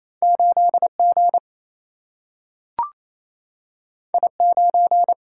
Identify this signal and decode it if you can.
Morse code, heard as a keyed tone, about 28 wpm, 700 Hz. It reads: 8 G I 9
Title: Single-Letter Single-Number 28wpm